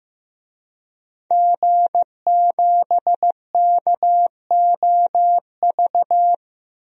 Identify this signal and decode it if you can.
Morse code, heard as a keyed tone, about 15 wpm, 700 Hz. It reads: G7KOV
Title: DXCC-100 Call Signs - 5 Characters - 15wpm